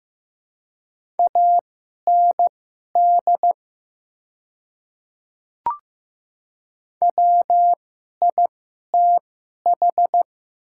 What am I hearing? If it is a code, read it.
Morse code, heard as a keyed tone, about 15 wpm, spaced as for 11 wpm, 700 Hz. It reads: AND WITH
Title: 2x Character Spacing - Top 100 Words - 15wpm